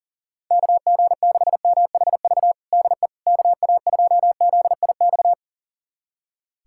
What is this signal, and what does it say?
Morse code, heard as a keyed tone, about 40 wpm, 700 Hz. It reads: KG6MHV DE KA2ZIX
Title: USA Callsign DE Callsign 40wpm